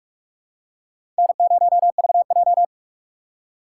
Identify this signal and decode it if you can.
Morse code, heard as a keyed tone, about 44 wpm, 700 Hz. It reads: N0VJ
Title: USA Call Signs - 3x Character Spacing - 45wpm